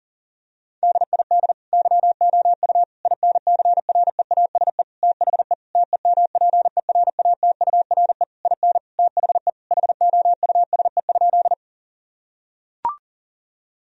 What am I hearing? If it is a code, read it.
Morse code, heard as a keyed tone, about 40 wpm, 700 Hz. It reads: DID YOU INCREASE THE TEMPERATURE IN THE HOUSE?
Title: Sentences from Top 1000 Words - Review Part 4 - 40wpm